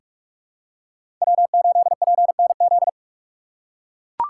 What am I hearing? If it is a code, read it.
Morse code, heard as a keyed tone, about 42 wpm, 700 Hz. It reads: W8PNZ
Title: USA Call Signs - 5 Characters - 6x Character Spacing - 45wpm